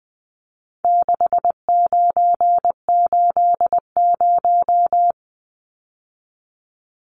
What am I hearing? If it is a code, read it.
Morse code, heard as a keyed tone, about 20 wpm, 700 Hz. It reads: 6980
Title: Numbers - 4-Digit 20wpm